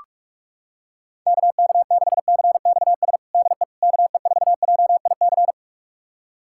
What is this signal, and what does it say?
Morse code, heard as a keyed tone, about 46 wpm, 700 Hz. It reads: KK6CXS DE KE4JIC
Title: USA Callsign DE Callsign - 6 Characters - 45wpm